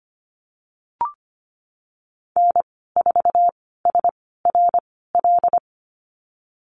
Morse code, heard as a keyed tone, about 25 wpm, spaced as for 16 wpm, 700 Hz. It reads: N4SRL
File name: Lesson 15g - Number 5 - Call Signs 25wpm